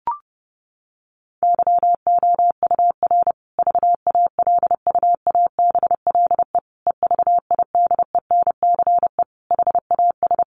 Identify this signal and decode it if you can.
Morse code, heard as a keyed tone, about 30 wpm, 700 Hz. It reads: YOUR VALUABLE EVIDENCE HAS
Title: Sentences from Top 1300 Words 30wpm